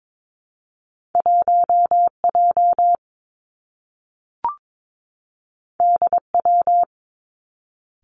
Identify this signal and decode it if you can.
Morse code, heard as a keyed tone, about 22 wpm, 700 Hz. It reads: 1 J D W
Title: Two Challenging Letter-Number Combinations 22wpm